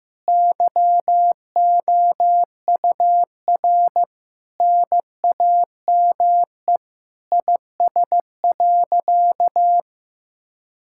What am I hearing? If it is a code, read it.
Morse code, heard as a keyed tone, about 15 wpm, 700 Hz. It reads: YOUR NAME IS.
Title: Sentences from Top 1000 Words - Review Part 6 - 15wpm